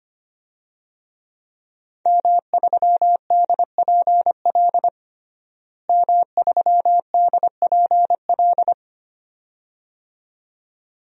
Morse code, heard as a keyed tone, about 25 wpm, 700 Hz. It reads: M3DPL M3DPL
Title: DXCC-100 Call Signs - 5 Characters - Repeated 2x - 25wpm